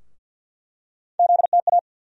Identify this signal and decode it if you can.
Morse code, heard as a keyed tone, about 47 wpm, 700 Hz. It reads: ZTA
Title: USA Call Sign Postfixes - 3 Letters 50wpm